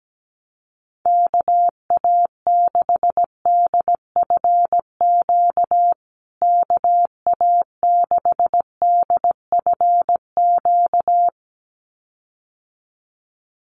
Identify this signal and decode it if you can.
Morse code, heard as a keyed tone, about 17 wpm, 700 Hz. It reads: KA6DFQ KA6DFQ
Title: USA Call Signs Repeated 2x 17wpm